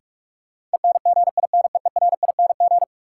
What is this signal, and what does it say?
Morse code, heard as a keyed tone, about 46 wpm, 700 Hz. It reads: ENGINEERING
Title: Sentences from Top 1500 Words - New Word Drill Alt - 45wpm